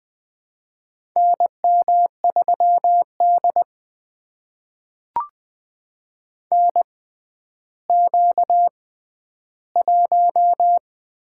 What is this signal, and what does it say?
Morse code, heard as a keyed tone, about 20 wpm, spaced as for 11 wpm, 700 Hz. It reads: NM3D NQ1
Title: USA Call Signs - 4 Characters - 6x Character Spacing - 20wpm